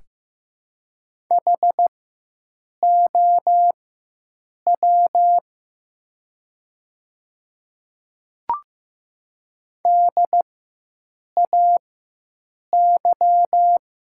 Morse code, heard as a keyed tone, about 15 wpm, spaced as for 7 wpm, 700 Hz. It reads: HOW DAY
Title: 4x Character Spacing - Top 100 Words - 15wpm